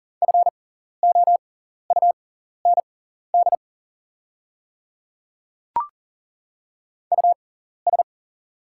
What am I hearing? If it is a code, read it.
Morse code, heard as a keyed tone, about 41 wpm, spaced as for 14 wpm, 690 Hz. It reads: FOUND US
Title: 6x Character Spacing - Top 300 Words - 40wpm